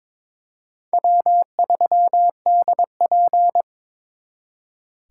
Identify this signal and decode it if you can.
Morse code, heard as a keyed tone, about 26 wpm, 700 Hz. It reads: W3DP
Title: Speed Racing - USA 1x2 Call Signs 22wpm